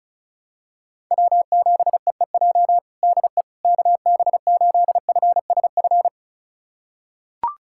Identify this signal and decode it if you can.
Morse code, heard as a keyed tone, about 36 wpm, 690 Hz. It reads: W7EEJ DE KB8FSF
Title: USA Callsign DE Callsign 35wpm